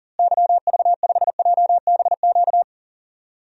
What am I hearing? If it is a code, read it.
Morse code, heard as a keyed tone, about 40 wpm, 700 Hz. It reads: YV5JBQ
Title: DXCC-100 Call Signs - 6 Characters - Repeated 2x - 40wpm